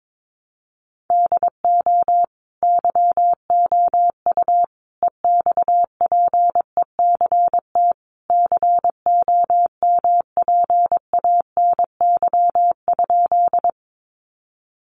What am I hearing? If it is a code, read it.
Morse code, heard as a keyed tone, about 22 wpm, 700 Hz. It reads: DO YOU EXPECT COMPANY?
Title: Sentences from Top 1100 Words 22wpm